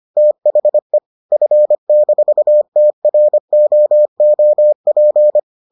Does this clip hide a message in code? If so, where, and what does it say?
Yes: Morse code from 0.2 to 5.4 s, heard as a keyed tone, about 25 wpm, 600 Hz. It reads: THE F-TROOP